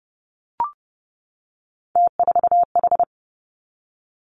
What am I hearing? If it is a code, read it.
Morse code, heard as a keyed tone, about 30 wpm, 700 Hz. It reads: T4H
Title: Three Challenging Letter-Number Combinations 30wpm